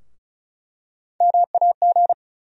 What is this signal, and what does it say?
Morse code, heard as a keyed tone, about 34 wpm, 700 Hz. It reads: MAG
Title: USA Call Sign Postfixes - 3 Letters 35wpm